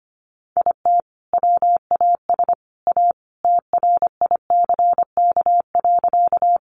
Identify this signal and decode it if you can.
Morse code, heard as a keyed tone, about 25 wpm, 700 Hz. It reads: IT WAS A TRICK.
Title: Sentences from Top 1500 Words 25wpm